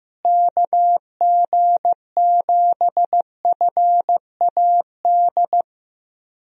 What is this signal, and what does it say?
Morse code, heard as a keyed tone, about 15 wpm, 700 Hz. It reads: KG7FAD
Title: USA Call Signs - 6 Characters - Repeated 2x - 15wpm